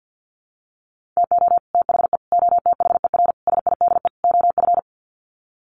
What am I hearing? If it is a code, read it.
Morse code, heard as a keyed tone, about 50 wpm, 700 Hz. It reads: TO THE OTHER SIDE OF
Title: Sets of 5 Words Encore 50wpm